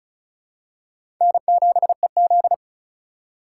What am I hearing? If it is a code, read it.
Morse code, heard as a keyed tone, about 36 wpm, 700 Hz. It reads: N7EZ
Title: USA Call Signs - 4 Characters - 4x Character Spacing - 35wpm